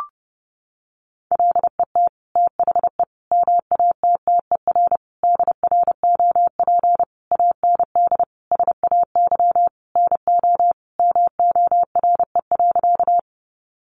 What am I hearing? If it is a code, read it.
Morse code, heard as a keyed tone, about 30 wpm, 700 Hz. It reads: LET THE MATTER DROP AND SAY NO MORE.
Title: Sentences from Top 1400 Words 30wpm